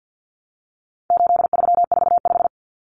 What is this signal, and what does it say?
Morse code, heard as a keyed tone, about 50 wpm, 700 Hz. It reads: 8345
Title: Common Cut Numbers - 4-Digit 50wpm